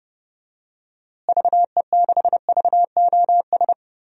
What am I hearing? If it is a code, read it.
Morse code, heard as a keyed tone, about 30 wpm, 700 Hz. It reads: VE6VOS
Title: DXCC-100 Call Signs - 6 Characters - Repeated 2x - 30wpm